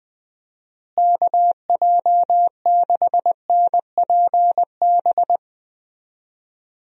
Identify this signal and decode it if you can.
Morse code, heard as a keyed tone, about 20 wpm, 700 Hz. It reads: KJ6NPB
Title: USA Call Signs - 6 Characters - Repeated 3x - 20wpm